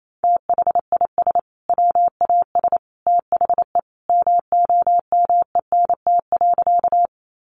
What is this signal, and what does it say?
Morse code, heard as a keyed tone, about 28 wpm, 700 Hz. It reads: THIS WAS THE MOMENT.
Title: Sentences from Top 1000 Words 28wpm